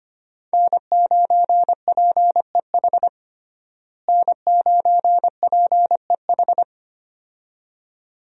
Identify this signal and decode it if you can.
Morse code, heard as a keyed tone, about 25 wpm, 700 Hz. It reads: N9PEH N9PEH
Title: USA Call Signs - 5 Characters - Repeated 2x - 25wpm